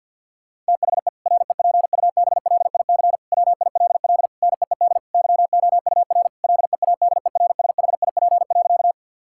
Morse code, heard as a keyed tone, about 50 wpm, 700 Hz. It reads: THE REPUBLIC WILL NEED YOUR LEADERSHIP.
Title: Sentences from Top 1600 Words 50wpm